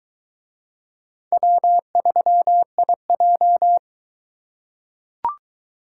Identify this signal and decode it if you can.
Morse code, heard as a keyed tone, about 20 wpm, 700 Hz. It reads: W3IJ